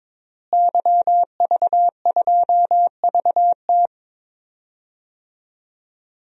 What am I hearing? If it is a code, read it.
Morse code, heard as a keyed tone, about 22 wpm, 700 Hz. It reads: YV2VT
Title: DXCC-100 Call Signs - 5 Characters - Repeated 3x - 22wpm